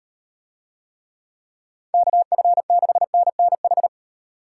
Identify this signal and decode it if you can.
Morse code, heard as a keyed tone, about 38 wpm, 700 Hz. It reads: KF6NNH